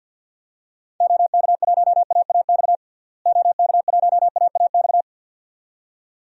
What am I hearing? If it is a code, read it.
Morse code, heard as a keyed tone, about 46 wpm, 700 Hz. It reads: OK1AAX OK1AAX
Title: DXCC-100 Call Signs - 6 Characters - Repeated 2x - 50wpm